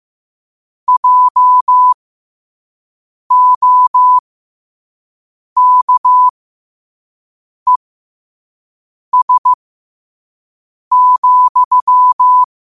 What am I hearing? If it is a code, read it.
Morse code, heard as a keyed tone, about 15 wpm, spaced as for 5 wpm, 1000 Hz. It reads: JOKES,
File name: 5wpm